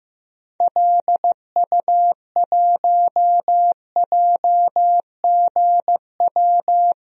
Morse code, heard as a keyed tone, about 15 wpm, 700 Hz. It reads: LU1JGW
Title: DXCC-100 Call Signs - 6 Characters - 15wpm